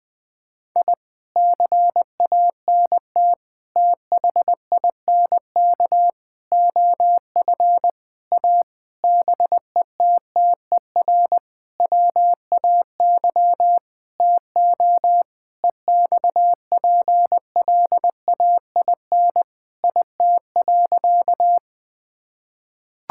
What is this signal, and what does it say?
Morse code, heard as a keyed tone, about 19 wpm, 700 Hz. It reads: I CANT THINK OF A BETTER WAY TO EXPLAIN IT.
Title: Sentences with 8-Word Phrases - 20wpm